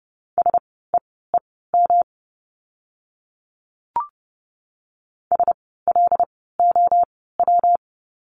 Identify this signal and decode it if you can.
Morse code, heard as a keyed tone, about 30 wpm, spaced as for 17 wpm, 700 Hz. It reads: SEEM SLOW
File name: course / 30:10wpm / 115 Lesson 19f - Letter w - Words 30wpm